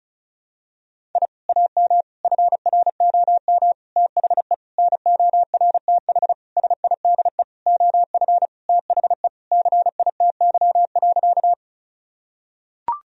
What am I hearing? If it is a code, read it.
Morse code, heard as a keyed tone, about 36 wpm, 700 Hz. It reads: I AM FROM THE NORTH SIDE OF THE CITY.
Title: Sentences from Top 1000 Words - Review Part 1 - 35wpm